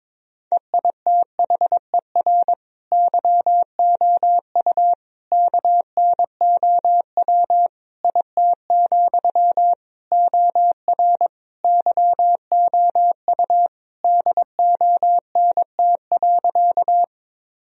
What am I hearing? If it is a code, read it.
Morse code, heard as a keyed tone, about 22 wpm, 700 Hz. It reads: EITHER YOU KNOW IT, OR YOU DONT.
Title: Sentences from Top 1000 Words - Review Part 4 - 22wpm